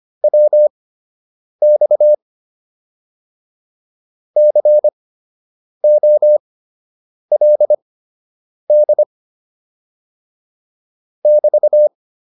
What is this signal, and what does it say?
Morse code, heard as a keyed tone, about 25 wpm, spaced as for 8 wpm, 600 Hz. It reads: WX COLD <BT>